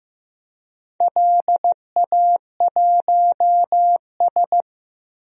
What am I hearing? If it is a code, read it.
Morse code, heard as a keyed tone, about 15 wpm, 700 Hz. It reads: LA1S
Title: DXCC-100 Call Signs - 4 Characters - Repeated 2x - 15wpm